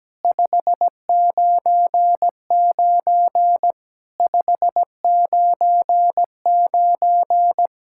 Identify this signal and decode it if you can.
Morse code, heard as a keyed tone, about 17 wpm, 700 Hz. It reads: 599 599
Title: Practice QSO - Exchange 1 - Beginning - 17wpm